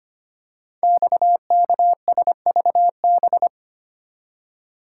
Morse code, heard as a keyed tone, about 25 wpm, 700 Hz. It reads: XKSVB
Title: Code Groups - 5 Letters - 25wpm